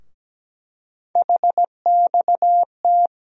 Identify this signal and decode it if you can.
Morse code, heard as a keyed tone, about 17 wpm, 700 Hz. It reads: HXT
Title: USA Call Sign Postfixes - 3 Letters 17wpm